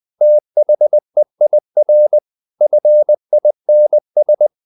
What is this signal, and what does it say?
Morse code, heard as a keyed tone, about 20 wpm, 600 Hz. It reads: THEIR FINS
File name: the old man and the sea / the old man and the sea part 1 20wpm